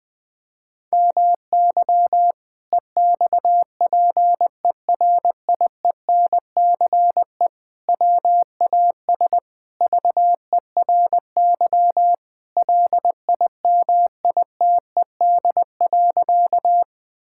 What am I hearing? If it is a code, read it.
Morse code, heard as a keyed tone, about 20 wpm, 700 Hz. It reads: MY EXPERIENCE WAS VERY LIMITED.
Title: Sentences from Top 1000 Words - Review Part 6 - 20wpm